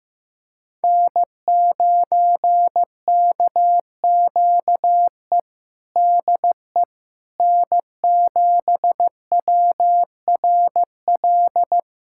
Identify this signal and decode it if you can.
Morse code, heard as a keyed tone, about 15 wpm, 700 Hz. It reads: N9KQE DE N7WRL
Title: USA Callsign DE Callsign - 5 Characters - 15wpm